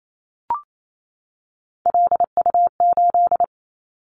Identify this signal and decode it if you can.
Morse code, heard as a keyed tone, about 28 wpm, 700 Hz. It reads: LU8
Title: Callsign Prefixes for Top 100 DXCC - 28wpm